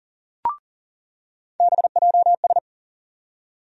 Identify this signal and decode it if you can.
Morse code, heard as a keyed tone, about 40 wpm, 700 Hz. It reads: BJS
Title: USA Call Sign Postfixes - 3 Letters 40wpm